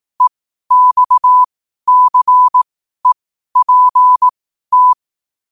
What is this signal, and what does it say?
Morse code, heard as a keyed tone, about 18 wpm, spaced as for 12 wpm, 1000 Hz. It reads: EXCEPT
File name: qotdfiles / 12wpm